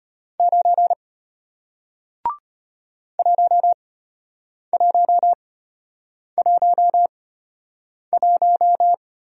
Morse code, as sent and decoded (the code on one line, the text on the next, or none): ----. / .---- .---- .---- .----
9 1111